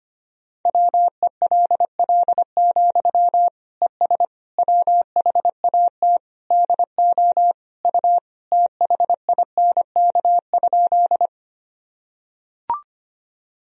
Morse code, heard as a keyed tone, about 25 wpm, 700 Hz. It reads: WELL, ES WHAT DO U THINK?
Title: Sentences from Top 200 Words and Abbreviations 25wpm